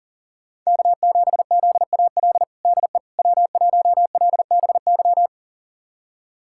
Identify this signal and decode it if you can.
Morse code, heard as a keyed tone, about 37 wpm, 700 Hz. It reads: K7ZAL DE W1LBY
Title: USA Callsign DE Callsign - 5 Characters - 40wpm